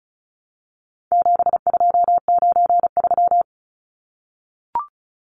Morse code, heard as a keyed tone, about 34 wpm, 700 Hz. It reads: 7293
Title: Numbers - 4-Digit 35wpm